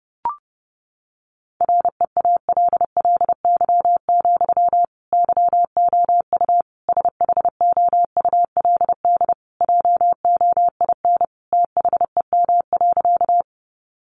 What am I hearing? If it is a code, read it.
Morse code, heard as a keyed tone, about 30 wpm, 700 Hz. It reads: REALLY, YOU SHOULD JOIN THEM.
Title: Sentences from Top 1400 Words 30wpm